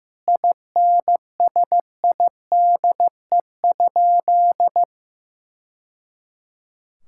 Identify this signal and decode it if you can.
Morse code, heard as a keyed tone, about 15 wpm, 700 Hz. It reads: INSIDE?
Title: Sentences from Top 1000 Words - Review Part 6 - 15wpm